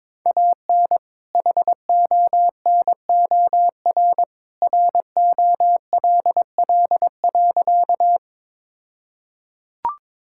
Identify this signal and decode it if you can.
Morse code, heard as a keyed tone, about 21 wpm, 700 Hz. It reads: AN HONOR ROLL.
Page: morse-code-ninja-practice-22wpm